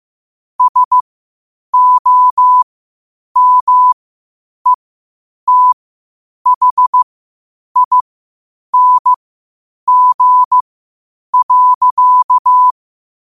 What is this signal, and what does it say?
Morse code, heard as a keyed tone, about 15 wpm, spaced as for 8 wpm, 1000 Hz. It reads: SOMETHING.
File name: qotdfiles / 8wpm